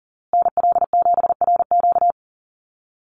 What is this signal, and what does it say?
Morse code, heard as a keyed tone, about 40 wpm, 700 Hz. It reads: NL7RQ
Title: USA Call Signs Repeated 3x 40wpm